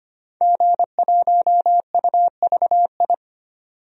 Morse code, heard as a keyed tone, about 25 wpm, 700 Hz. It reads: G1UVI
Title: DXCC-100 Call Signs - 5 Characters - 25wpm